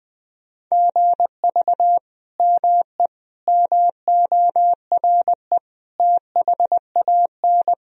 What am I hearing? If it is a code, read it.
Morse code, heard as a keyed tone, about 20 wpm, 700 Hz. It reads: GV ME MORE THAN
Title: Sentences from Top 200 Words and Abbreviations 20wpm